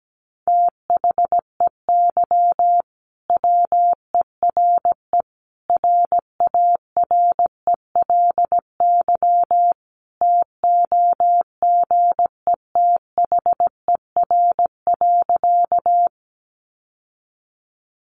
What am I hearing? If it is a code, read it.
Morse code, heard as a keyed tone, about 17 wpm, 700 Hz. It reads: THEY WERE RARELY TOGETHER.
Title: Sentences from Top 1400 Words 17wpm